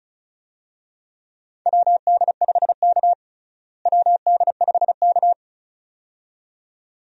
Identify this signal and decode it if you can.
Morse code, heard as a keyed tone, about 34 wpm, 700 Hz. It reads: WD5K WD5K
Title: USA Call Signs - 4 Characters - Repeated 2x - 35wpm